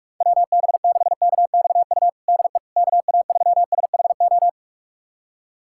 Morse code, heard as a keyed tone, about 46 wpm, 700 Hz. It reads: WB6KXU DE KA3SHO
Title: USA Callsign DE Callsign 45wpm